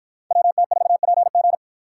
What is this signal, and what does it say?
Morse code, heard as a keyed tone, about 53 wpm, 700 Hz. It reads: WT4PG